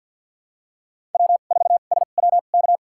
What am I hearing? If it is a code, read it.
Morse code, heard as a keyed tone, about 52 wpm, spaced as for 37 wpm, 700 Hz. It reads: W4SWK
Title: USA Call Signs - 5 Characters - 2x Character Spacing - 50wpm